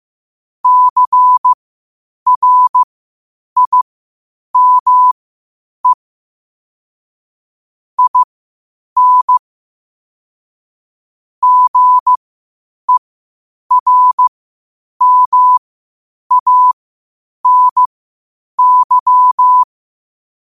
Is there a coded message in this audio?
Yes: Morse code from 0.6 to 19.6 s, heard as a keyed tone, about 15 wpm, spaced as for 8 wpm, 1000 Hz. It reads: CRIME IN GERMANY